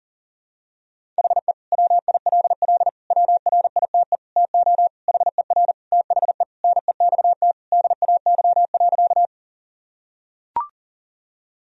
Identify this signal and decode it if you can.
Morse code, heard as a keyed tone, about 40 wpm, 700 Hz. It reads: HE WILL WRITE TO HER THE NEXT DAY.
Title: Sentences from Top 1000 Words - Review Part 1 - 40wpm